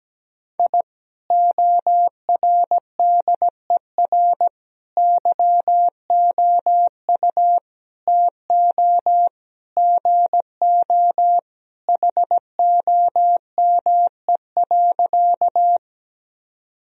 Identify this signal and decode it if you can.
Morse code, heard as a keyed tone, about 17 wpm, 700 Hz. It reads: I ORDER YOU TO GO HOME.
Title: Sentences from Top 1000 Words - Review Part 2 - 17wpm